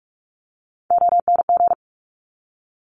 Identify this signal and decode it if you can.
Morse code, heard as a keyed tone, about 46 wpm, 690 Hz. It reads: ONG
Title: Top Three Letter Combinations in Words 45wpm